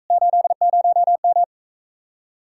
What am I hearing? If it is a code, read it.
Morse code, heard as a keyed tone, about 45 wpm, 700 Hz. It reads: 80M